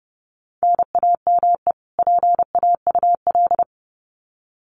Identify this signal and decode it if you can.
Morse code, heard as a keyed tone, about 30 wpm, 700 Hz. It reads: NAME PAUL
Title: Common QSO Phrases 30wpm